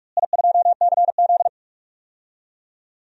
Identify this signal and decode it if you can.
Morse code, heard as a keyed tone, about 41 wpm, 700 Hz. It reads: I2CZ
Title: DXCC-100 Call Signs - 4 Characters - Repeated 2x - 45wpm